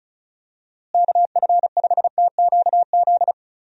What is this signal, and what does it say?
Morse code, heard as a keyed tone, about 36 wpm, 700 Hz. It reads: KF5TQZ
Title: USA Call Signs - 6 Characters - Repeated 2x - 35wpm